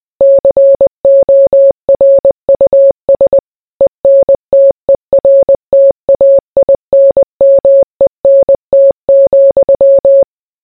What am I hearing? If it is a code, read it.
Morse code, heard as a keyed tone, about 20 wpm, 550 Hz. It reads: CORUS ENTERTAINMENT,